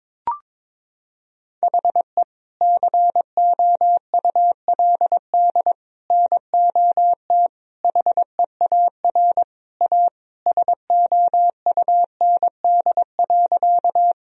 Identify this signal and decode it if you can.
Morse code, heard as a keyed tone, about 22 wpm, 700 Hz. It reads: HE COULD NOT HEAR A SOUND.
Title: Sentences from Top 1000 Words - Review Part 1 - 22wpm